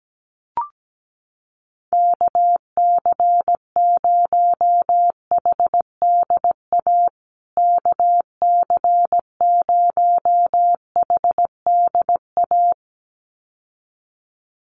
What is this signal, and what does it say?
Morse code, heard as a keyed tone, about 17 wpm, 700 Hz. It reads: KC0HDA KC0HDA
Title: USA Call Signs Repeated 2x 17wpm